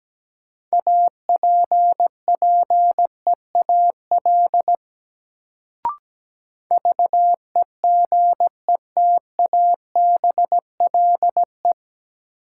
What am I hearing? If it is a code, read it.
Morse code, heard as a keyed tone, about 17 wpm, 700 Hz. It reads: APPEAL VEGETABLE
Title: Sentences from Top 1300 Words - New Word Drill Alt - 17wpm